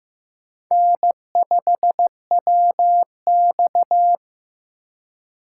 Morse code, heard as a keyed tone, about 18 wpm, 700 Hz. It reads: N5WX
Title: Speed Racing - USA Call Signs 15wpm